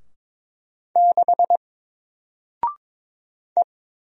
Morse code, heard as a keyed tone, about 22 wpm, 700 Hz. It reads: 6 E
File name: Single-Letter-Number-22wpm